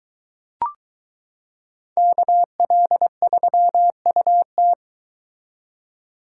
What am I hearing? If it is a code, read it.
Morse code, heard as a keyed tone, about 22 wpm, 700 Hz. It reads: KL3UT